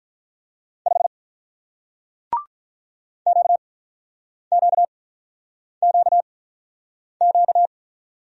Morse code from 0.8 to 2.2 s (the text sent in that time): 5